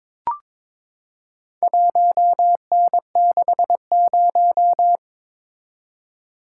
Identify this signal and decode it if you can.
Morse code, heard as a keyed tone, about 22 wpm, 700 Hz. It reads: 1N60
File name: practice / 22wpm / Four Letter-Number Combinations - 22wpm